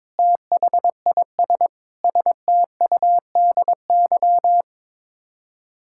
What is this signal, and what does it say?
Morse code, heard as a keyed tone, about 22 wpm, 700 Hz. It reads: THIS STUDY
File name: practice / 22wpm / Build A Phrase - 5-Word Phrases - 22wpm